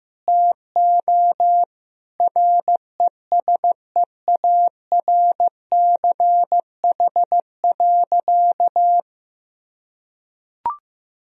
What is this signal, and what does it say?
Morse code, heard as a keyed tone, about 15 wpm, 700 Hz. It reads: TO RESEARCH.
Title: Sentences from Top 1000 Words - Review Part 4 - 15wpm